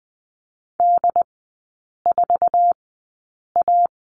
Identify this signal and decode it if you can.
Morse code, heard as a keyed tone, about 20 wpm, spaced as for 8 wpm, 700 Hz. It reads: D4A
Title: 067 Lesson 12e - Letter d - Three Characters 20wpm